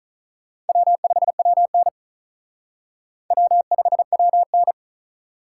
Morse code, heard as a keyed tone, about 43 wpm, 700 Hz. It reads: W5WN W5WN
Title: Speed Racing - USA 1x2 Call Signs 35wpm